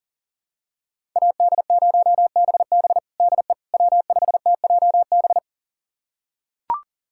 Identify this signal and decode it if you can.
Morse code, heard as a keyed tone, about 40 wpm, 700 Hz. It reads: AD0BB DE W5TJB
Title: USA Callsign DE Callsign - 5 Characters - 40wpm